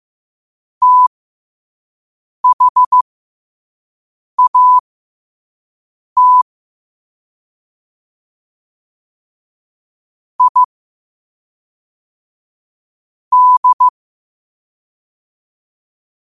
Morse code, heard as a keyed tone, about 15 wpm, spaced as for 5 wpm, 1000 Hz. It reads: THAT I D
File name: qotdfiles / 5wpm